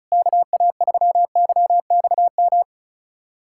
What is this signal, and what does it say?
Morse code, heard as a keyed tone, about 36 wpm, 700 Hz. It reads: KA3YXM
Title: USA Callsign DE Callsign - 6 Characters - 35wpm